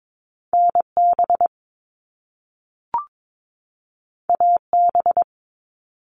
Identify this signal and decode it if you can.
Morse code, heard as a keyed tone, about 22 wpm, 700 Hz. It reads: NB AB
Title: Canadian Provinces and Territories 22wpm